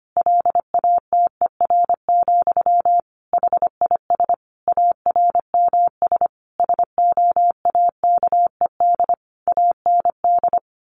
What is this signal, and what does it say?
Morse code, heard as a keyed tone, about 25 wpm, 700 Hz. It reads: LATER, HIS ARMS SOAKED AND
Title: A Sound Of Thunder 25wpm